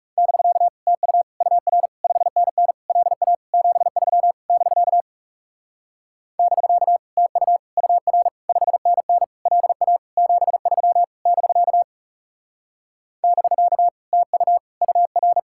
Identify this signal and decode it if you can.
Morse code, heard as a keyed tone, about 36 wpm, 700 Hz. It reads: <BK> TU UR 5NN LA 73 <BK> <BK> TU UR 5NN LA 73 <BK> <BK> TU UR
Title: Speed-Racing Common QSO Phrases - 30wpm